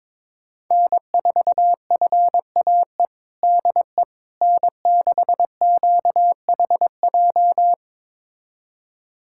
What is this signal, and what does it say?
Morse code, heard as a keyed tone, about 22 wpm, 700 Hz. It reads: N4FAE DE N6QHJ
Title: USA Callsign DE Callsign - 5 Characters - 22wpm